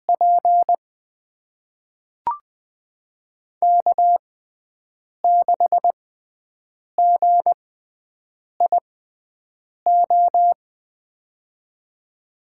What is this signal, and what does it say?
Morse code, heard as a keyed tone, about 21 wpm, spaced as for 7 wpm, 700 Hz. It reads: P K6GIO
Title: USA Call Signs - 5 Characters - 6x Character Spacing - 20wpm